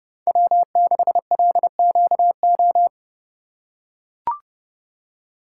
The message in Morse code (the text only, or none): W6LQO